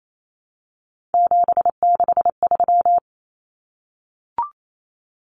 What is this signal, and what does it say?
Morse code, heard as a keyed tone, about 28 wpm, 700 Hz. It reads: 763
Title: Numbers - 3-Digit 28wpm